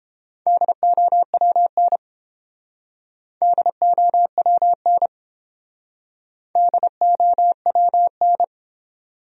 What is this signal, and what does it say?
Morse code, heard as a keyed tone, about 30 wpm, 700 Hz. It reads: DOWN DOWN DOWN